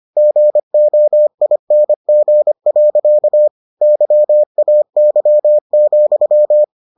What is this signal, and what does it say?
Morse code, heard as a keyed tone, about 25 wpm, 600 Hz. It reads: GOING. YAY,